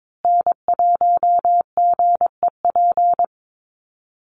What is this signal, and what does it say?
Morse code, heard as a keyed tone, about 22 wpm, 700 Hz. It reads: N1GEP
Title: USA Call Signs Repeated 3x 22wpm